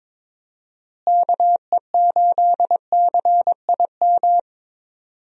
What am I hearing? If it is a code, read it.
Morse code, heard as a keyed tone, about 22 wpm, 700 Hz. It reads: KE8CIM
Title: USA Call Signs - 6 Characters - Repeated 3x - 22wpm